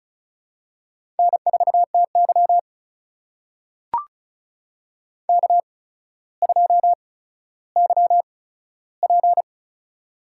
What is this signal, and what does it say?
Morse code, heard as a keyed tone, about 36 wpm, spaced as for 15 wpm, 700 Hz. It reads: N4TY K2YP
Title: USA Call Signs - 4 Characters - 8x Character Spacing - 35wpm